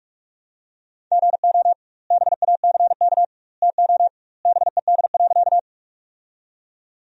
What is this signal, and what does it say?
Morse code, heard as a keyed tone, about 44 wpm, 700 Hz. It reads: GO BACK TO BED.
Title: 2x Word-Spacing - Sentences from Top 500 Words - 45wpm